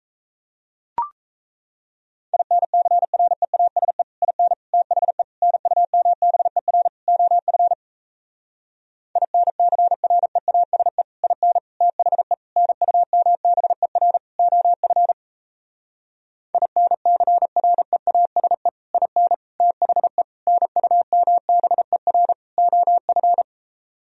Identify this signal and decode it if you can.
Morse code, heard as a keyed tone, about 37 wpm, 700 Hz. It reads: INCREASE IN THE NUMBER OF INCREASE IN THE NUMBER OF INCREASE IN THE NUMBER OF